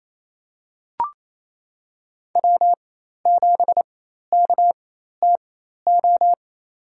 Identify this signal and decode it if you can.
Morse code, heard as a keyed tone, about 28 wpm, spaced as for 13 wpm, 700 Hz. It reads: W7KTO
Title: USA Call Signs - 5 Characters - 4x Character Spacing - 28wpm